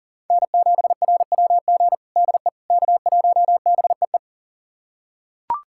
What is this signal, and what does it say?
Morse code, heard as a keyed tone, about 37 wpm, 700 Hz. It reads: N7RWG DE K1BEE
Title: USA Callsign DE Callsign - 5 Characters - 40wpm